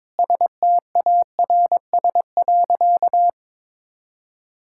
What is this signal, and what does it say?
Morse code, heard as a keyed tone, about 22 wpm, 700 Hz. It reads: STARS.
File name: practice / 22wpm / Sentences with 2-Word Phrases - 22wpm